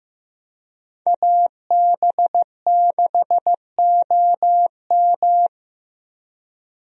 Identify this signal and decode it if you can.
Morse code, heard as a keyed tone, about 15 wpm, 700 Hz. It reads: AB6OM
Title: USA Call Signs - 5 Characters - 15wpm